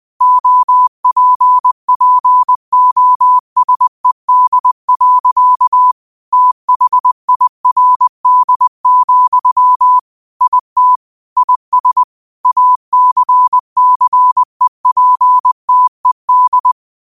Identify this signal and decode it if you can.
Morse code, heard as a keyed tone, about 20 wpm, 1000 Hz. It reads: OPPOSED. THIRD, IT IS ACCEPTED
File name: qotdfiles / 20wpm